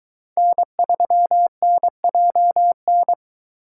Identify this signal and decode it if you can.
Morse code, heard as a keyed tone, about 22 wpm, 700 Hz. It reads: N3NJN